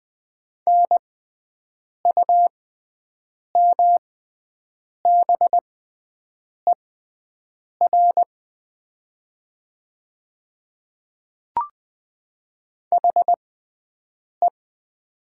Morse code, heard as a keyed tone, about 20 wpm, spaced as for 7 wpm, 700 Hz. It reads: NUMBER HE
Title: 6x Character Spacing - Top 100 Words - 20wpm